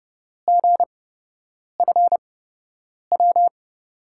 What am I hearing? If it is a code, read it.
Morse code, heard as a keyed tone, about 31 wpm, spaced as for 8 wpm, 700 Hz. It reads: GFW
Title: USA Call Signs - 5 Characters - 8x Character Spacing - 30wpm